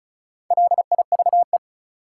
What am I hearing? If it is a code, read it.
Morse code, heard as a keyed tone, about 36 wpm, 700 Hz. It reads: LIVE
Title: Sentences from Top 200 Words - New Word Drill - 35wpm